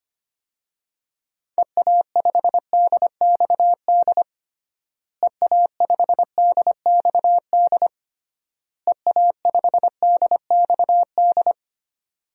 Morse code, heard as a keyed tone, about 25 wpm, 700 Hz. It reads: EA5DXD EA5DXD EA5DXD
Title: DXCC-100 Call Signs - 6 Characters - Repeated 3x - 25wpm